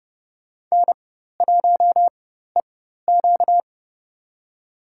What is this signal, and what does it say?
Morse code, heard as a keyed tone, about 31 wpm, spaced as for 14 wpm, 700 Hz. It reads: N1EQ
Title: USA Call Signs - 4 Characters - 4x Character Spacing - 30wpm